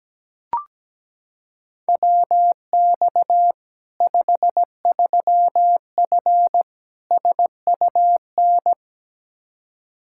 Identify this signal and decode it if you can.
Morse code, heard as a keyed tone, about 17 wpm, 700 Hz. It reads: WX 53F SUN
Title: Practice QSO - Exchange 2 - Info - 17wpm